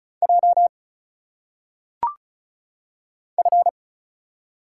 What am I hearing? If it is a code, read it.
Morse code, heard as a keyed tone, about 36 wpm, 700 Hz. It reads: J F
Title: Single Letters - Phonetic Pronunciation 35wpm